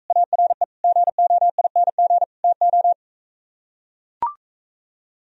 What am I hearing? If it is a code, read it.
Morse code, heard as a keyed tone, about 44 wpm, 700 Hz. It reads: ARE GOING TO